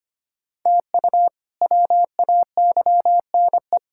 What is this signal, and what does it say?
Morse code, heard as a keyed tone, about 25 wpm, 700 Hz. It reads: TU WAYNE
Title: K1USN SST QSO Practice - 25wpm